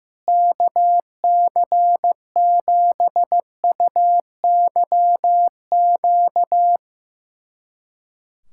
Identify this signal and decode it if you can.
Morse code, heard as a keyed tone, about 15 wpm, 700 Hz. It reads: KC7UYQ
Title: USA Callsign DE Callsign - 6 Characters - 15wpm